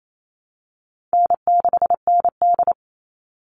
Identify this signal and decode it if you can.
Morse code, heard as a keyed tone, about 28 wpm, 700 Hz. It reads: N6ND
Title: All Cut Numbers - 4-Digit 28wpm